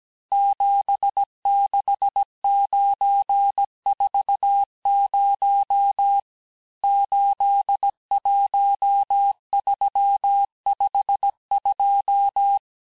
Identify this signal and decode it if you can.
Morse code, heard as a keyed tone, about 17 wpm, 800 Hz. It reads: 76940 81352
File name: cours-du-lundi files / cours-1587-texte-chiffre-vitesse-16MPM